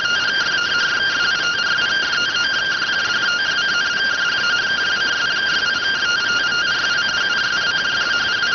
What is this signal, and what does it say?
Two-tone teleprinter signal, about 170 Hz shift.